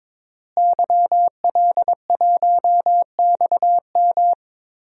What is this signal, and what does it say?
Morse code, heard as a keyed tone, about 21 wpm, 700 Hz. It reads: YL1XM
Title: DXCC-100 Call Signs - 5 Characters - Repeated 2x - 22wpm